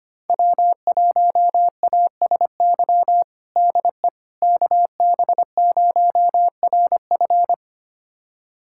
Morse code, heard as a keyed tone, about 25 wpm, 700 Hz. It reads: W1ASY DE KB0RF
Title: USA Callsign DE Callsign - 5 Characters - 25wpm